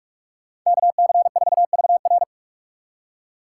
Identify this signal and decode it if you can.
Morse code, heard as a keyed tone, about 47 wpm, 690 Hz. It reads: KC4VR
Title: USA Call Signs - 5 Characters - 6x Character Spacing - 45wpm